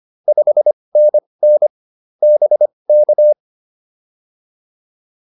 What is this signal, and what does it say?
Morse code, heard as a keyed tone, about 25 wpm, spaced as for 20 wpm, 600 Hz. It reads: 5NN BK